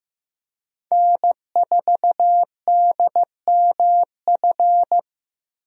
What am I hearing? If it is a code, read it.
Morse code, heard as a keyed tone, about 15 wpm, 700 Hz. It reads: N4DMF
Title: USA Call Signs - 5 Characters - 15wpm